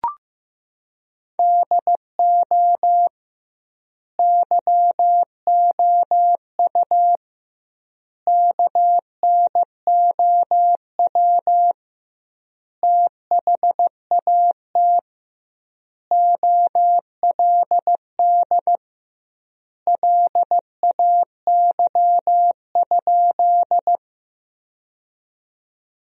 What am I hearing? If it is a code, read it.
Morse code, heard as a keyed tone, about 15 wpm, 700 Hz. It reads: DO YOU KNOW THAT OLD LAY?
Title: 2x Word-Spacing - Sentences from Top 400 Words - 15wpm